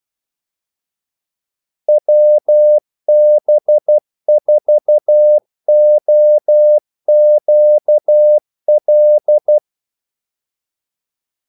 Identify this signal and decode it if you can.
Morse code, heard as a keyed tone, about 12 wpm, 600 Hz. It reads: WB4OQL
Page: 2 ekstra ordmellemrum mellem kaldesignalerne